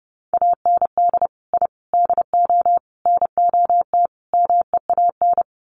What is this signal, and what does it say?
Morse code, heard as a keyed tone, about 30 wpm, 700 Hz. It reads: AND I DO NOT MEAN
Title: Sets of 5 Words Encore2 30wpm